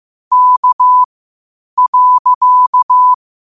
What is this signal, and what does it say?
Morse code, heard as a keyed tone, about 15 wpm, spaced as for 8 wpm, 1000 Hz. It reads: K.
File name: qotdfiles / 8wpm